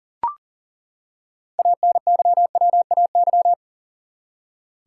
Morse code, heard as a keyed tone, about 40 wpm, 700 Hz. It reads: ANYWAY
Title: Six Letter Words - 40wpm